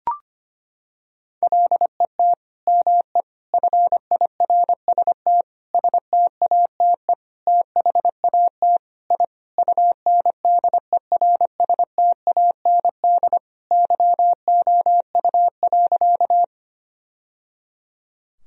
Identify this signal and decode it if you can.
Morse code, heard as a keyed tone, about 25 wpm, 700 Hz. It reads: LET ME FIRST STATE THAT I UNDERSTAND YOU.
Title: Sentences from Top 1000 Words - Review Part 2 - 25wpm